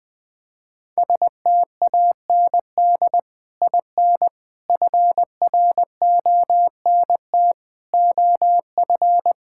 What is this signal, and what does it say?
Morse code, heard as a keyed tone, about 20 wpm, 700 Hz. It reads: STAND IN FRONT OF
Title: Sentences with 3-Word Phrases - 20wpm